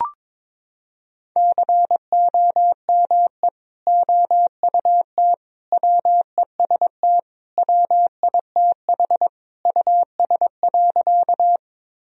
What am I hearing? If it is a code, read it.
Morse code, heard as a keyed tone, about 22 wpm, 700 Hz. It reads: COME OUT WEST WITH US.
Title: Sentences from Top 1000 Words - Review Part 2 - 22wpm